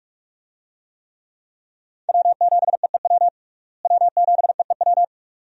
Morse code, heard as a keyed tone, about 46 wpm, 700 Hz. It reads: W7EEW W7EEW
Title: USA Call Signs - 5 Characters - Repeated 2x - 45wpm